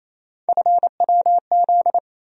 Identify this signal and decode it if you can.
Morse code, heard as a keyed tone, about 28 wpm, 700 Hz. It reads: FWZ
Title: USA Call Sign Postfixes - 3 Letters 28wpm